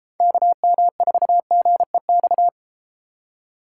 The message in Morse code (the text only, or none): KM4GEX